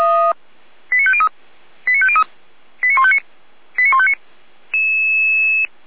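Five-tone selective call (ZVEI), heard as quick runs of five tones, tones of about 70 ms. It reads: 89503 89503 89159 89159
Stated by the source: ZVEI five-tone call 89160 received by SDR, audio.f85360.feuerwehr-orlamuende.org fire brigade server.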